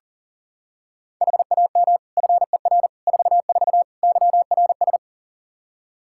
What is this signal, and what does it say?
Morse code, heard as a keyed tone, about 40 wpm, 700 Hz. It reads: HAM FER 44 YRS
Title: Practice QSO - Exchange 3 - More Info - 40wpm